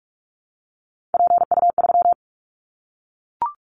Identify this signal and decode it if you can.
Morse code, heard as a keyed tone, about 44 wpm, 700 Hz. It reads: PU3
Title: Callsign Prefixes for Top 50 DXCC - 45wpm